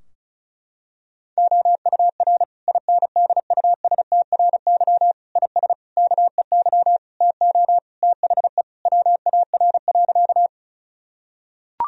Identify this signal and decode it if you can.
Morse code, heard as a keyed tone, about 36 wpm, 700 Hz. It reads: OUR INDUSTRY IS KEY TO THE WAR.
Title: Sentences from Top 1000 Words - Review Part 4 - 35wpm